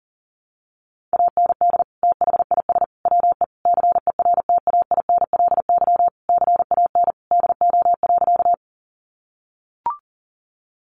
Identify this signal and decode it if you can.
Morse code, heard as a keyed tone, about 40 wpm, 700 Hz. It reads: AND THIS WE CERTAINLY CAN DO.
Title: Sentences from Top 900 Words 40wpm